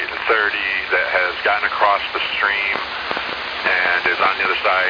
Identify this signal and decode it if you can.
Emergency Alert System (SAME) tones, heard as harsh buzzing bursts, with speech. It is heard from 0.8 to 4.1 s.